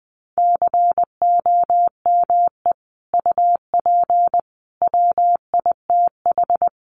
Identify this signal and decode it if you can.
Morse code, heard as a keyed tone, about 20 wpm, 700 Hz. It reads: COME UP WITH
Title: Sets of 3 Words 20wpm